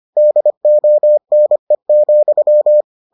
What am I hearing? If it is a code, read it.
Morse code, heard as a keyed tone, about 25 wpm, 600 Hz. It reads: DONE,